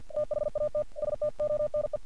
Morse code, heard as a keyed tone, about 49 wpm, 600 Hz. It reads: THAT STONE
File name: archived / CRSnet